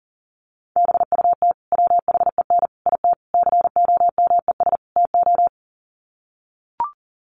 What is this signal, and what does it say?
Morse code, heard as a keyed tone, about 40 wpm, 700 Hz. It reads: BUT WHEN IT COMES TO
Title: Sets of 5 Words 40wpm